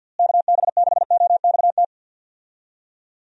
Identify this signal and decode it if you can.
Morse code, heard as a keyed tone, about 51 wpm, 700 Hz. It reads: KB6OXT
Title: USA Call Signs - 6 Characters - Repeated 2x - 50wpm